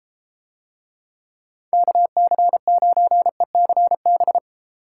Morse code, heard as a keyed tone, about 32 wpm, 700 Hz. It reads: KC9ECB